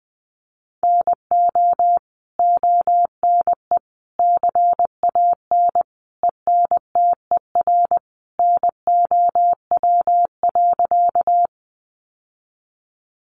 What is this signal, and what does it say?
Morse code, heard as a keyed tone, about 19 wpm, 700 Hz. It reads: NO ONE CAN ENTER NOW.
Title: Sentences from Top 1300 Words 20wpm